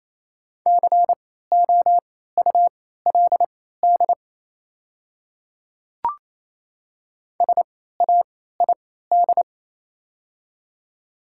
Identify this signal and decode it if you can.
Morse code, heard as a keyed tone, about 28 wpm, spaced as for 16 wpm, 700 Hz. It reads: COULD SAID
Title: 3x Character Spacing - Top 100 Words - 28wpm